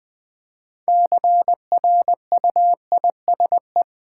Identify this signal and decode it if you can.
Morse code, heard as a keyed tone, about 20 wpm, 700 Hz. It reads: CRUISE
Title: Build A Word - 3 Letters At A Time - 6-Letter Words - 20wpm